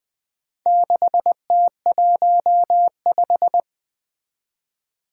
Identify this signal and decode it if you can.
Morse code, heard as a keyed tone, about 24 wpm, 700 Hz. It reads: 6T15
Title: Speed-Racing - Four Letter-Number Combinations - 20wpm